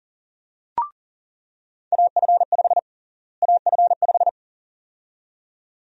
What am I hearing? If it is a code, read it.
Morse code, heard as a keyed tone, about 40 wpm, 700 Hz. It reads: AF5 AF5
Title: USA Call Sign Prefixes - Repeated 2x - 40wpm